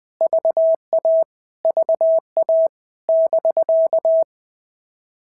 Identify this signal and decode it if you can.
Morse code, heard as a keyed tone, about 20 wpm, 650 Hz. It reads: VA VA <BK>
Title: POTA QSO Practice - 20wpm